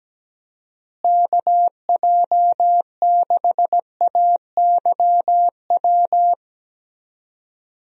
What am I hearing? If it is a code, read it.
Morse code, heard as a keyed tone, about 17 wpm, 700 Hz. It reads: KJ6AYW
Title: USA Call Signs - 6 Characters - Repeated 3x - 17wpm